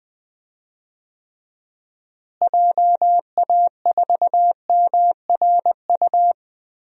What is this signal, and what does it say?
Morse code, heard as a keyed tone, about 21 wpm, 700 Hz. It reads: JA4MRU
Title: DXCC-100 Call Signs - 6 Characters - Repeated 2x - 20wpm